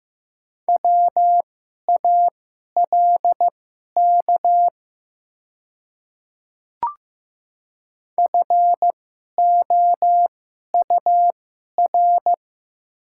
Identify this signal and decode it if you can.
Morse code, heard as a keyed tone, about 15 wpm, spaced as for 11 wpm, 700 Hz. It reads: WALK FOUR
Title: 2x Character Spacing - Top 300 Words - 15wpm